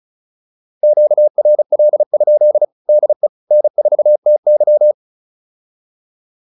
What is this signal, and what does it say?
Morse code, heard as a keyed tone, about 36 wpm, 600 Hz. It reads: QRL? DE N4TY